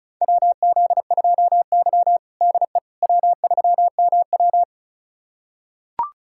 Morse code, heard as a keyed tone, about 36 wpm, 700 Hz. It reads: WZ2Y DE W3MW
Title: USA Callsign DE Callsign - 4 Characters - 35wpm